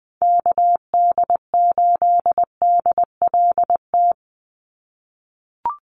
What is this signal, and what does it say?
Morse code, heard as a keyed tone, about 20 wpm, 700 Hz. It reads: KD8DLT
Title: USA Call Signs 20wpm